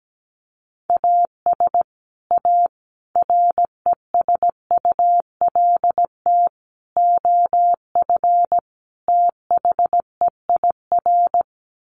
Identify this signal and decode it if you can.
Morse code, heard as a keyed tone, about 17 wpm, 700 Hz. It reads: AS A RESULT OF THEIR
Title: Sets of 5 Words Encore2 17wpm